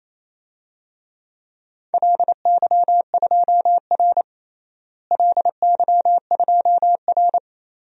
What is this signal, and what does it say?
Morse code, heard as a keyed tone, about 27 wpm, 700 Hz. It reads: LY2R LY2R
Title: DXCC-100 Call Signs - 4 Characters - Repeated 2x - 28wpm